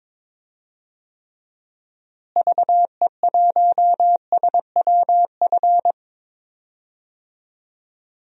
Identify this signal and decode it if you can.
Morse code, heard as a keyed tone, about 22 wpm, 700 Hz. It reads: VE1SWF
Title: DXCC-100 Call Signs - 6 Characters - Repeated 2x - 22wpm